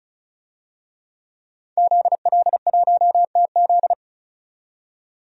Morse code, heard as a keyed tone, about 34 wpm, 700 Hz. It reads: ZL1TZ
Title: DXCC-100 Call Signs - 5 Characters - 35wpm